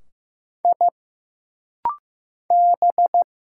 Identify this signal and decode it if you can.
Morse code, heard as a keyed tone, about 15 wpm, 700 Hz. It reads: I B
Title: Single Letters - Rapid-Fire 15wpm